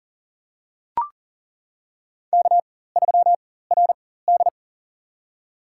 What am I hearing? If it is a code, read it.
Morse code, heard as a keyed tone, about 40 wpm, spaced as for 19 wpm, 700 Hz. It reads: K3RD
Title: USA Call Signs - 4x Character Spacing - 40wpm